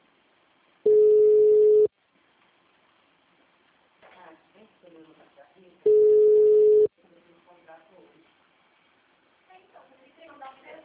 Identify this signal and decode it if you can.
Ringback tone (Europe, 425 Hz), with speech.
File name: out- -2001- - - .31519